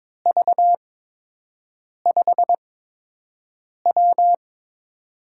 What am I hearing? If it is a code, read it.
Morse code, heard as a keyed tone, about 22 wpm, spaced as for 6 wpm, 700 Hz. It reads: V5W
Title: USA Call Signs - 4 Characters - 8x Character Spacing - 22wpm